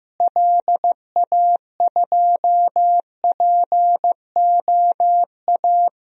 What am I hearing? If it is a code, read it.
Morse code, heard as a keyed tone, about 14 wpm, 700 Hz. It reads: LA2POA
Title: DXCC-100 Call Signs - 6 Characters - Repeated 2x - 15wpm